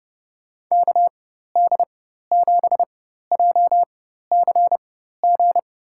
Morse code, heard as a keyed tone, about 30 wpm, spaced as for 14 wpm, 700 Hz. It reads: KD7JCG
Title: USA Call Signs - 6 Characters - 4x Character Spacing - 30wpm